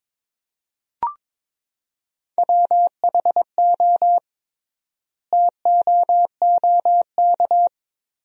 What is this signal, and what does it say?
Morse code, heard as a keyed tone, about 22 wpm, 700 Hz. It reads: WHO TOOK
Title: 3x Word-Spacing - Sentences from Top 300 Words - 22wpm